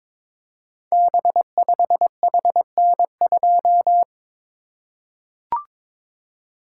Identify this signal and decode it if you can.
Morse code, heard as a keyed tone, about 22 wpm, 700 Hz. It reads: B5HN2
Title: Code Groups - 5 Letter:Numbers - 22wpm